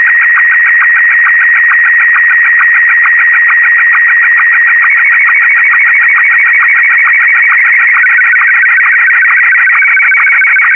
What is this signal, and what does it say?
SSTV (slow-scan television), Martin 1.